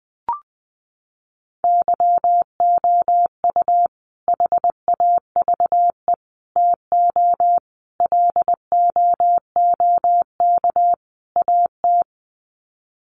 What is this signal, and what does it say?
Morse code, heard as a keyed tone, about 20 wpm, 700 Hz. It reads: YOU HAVE TO LOOK AT
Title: Sets of 5 Words Encore 20wpm